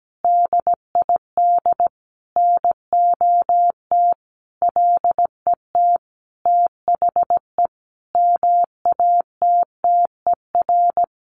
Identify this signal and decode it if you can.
Morse code, heard as a keyed tone, about 17 wpm, 700 Hz. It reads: DID NOT LET THE MATTER
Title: Sentences from Top 1400 Words 17wpm